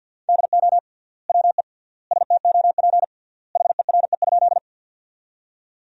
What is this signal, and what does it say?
Morse code, heard as a keyed tone, about 50 wpm, 700 Hz. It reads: DO WE STOP HERE?
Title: 3x Word-Spacing - Sentences from Top 300 Words - 50wpm